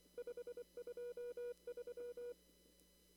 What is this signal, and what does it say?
Morse code, heard as a keyed tone, about 24 wpm, 480 Hz. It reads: 523
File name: CW / kt5j wpx cw-op-k5pi